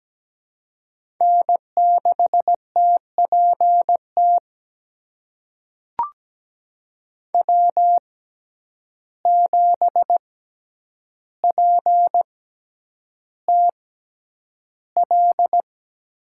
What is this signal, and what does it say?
Morse code, heard as a keyed tone, about 17 wpm, spaced as for 9 wpm, 700 Hz. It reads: N6TPT W7PTL
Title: USA Call Signs - 6x Character Spacing - 17wpm